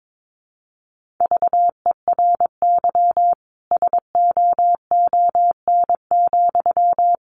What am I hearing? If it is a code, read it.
Morse code, heard as a keyed tone, about 22 wpm, 700 Hz. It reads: VERY SOON,
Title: Sentences from Top 1400 Words 22wpm